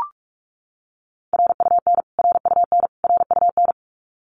Morse code, heard as a keyed tone, about 44 wpm, 700 Hz. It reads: RUN RUN RUN
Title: Top 500 Words Repeated 45wpm